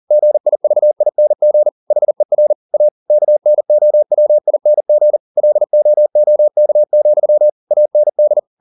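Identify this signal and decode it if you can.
Morse code, heard as a keyed tone, about 40 wpm, 600 Hz. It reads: GIVING HER A KNOWING LOOK, AND
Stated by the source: Morse code at 40 wpm, practice text, CW med Gnister.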